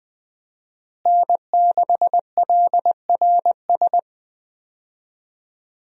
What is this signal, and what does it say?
Morse code, heard as a keyed tone, about 20 wpm, 700 Hz. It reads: N6LRS
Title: USA Call Signs - 5 Characters - Repeated 3x - 20wpm